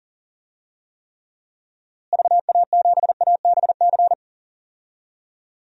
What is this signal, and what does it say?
Morse code, heard as a keyed tone, about 39 wpm, 700 Hz. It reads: VA7ABC
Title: DXCC-100 Call Signs - 6 Characters - Repeated 2x - 40wpm